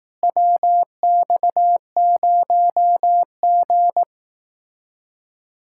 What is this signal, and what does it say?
Morse code, heard as a keyed tone, about 18 wpm, 700 Hz. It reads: WX0G
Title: Speed-Racing Common QSO Phrases Encore - 15wpm